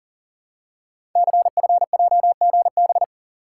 Morse code, heard as a keyed tone, about 42 wpm, 700 Hz. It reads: CFJGB
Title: Code Groups - 5 Letters - 40wpm